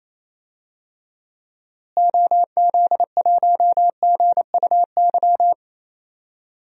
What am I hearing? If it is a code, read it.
Morse code, heard as a keyed tone, about 28 wpm, 700 Hz. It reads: OZ1GUY
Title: DXCC-100 Call Signs - 6 Characters - Repeated 3x - 28wpm